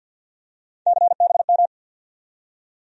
Morse code, heard as a keyed tone, about 53 wpm, 700 Hz. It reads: CBM
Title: USA Call Sign Postfixes - 3 Letters 50wpm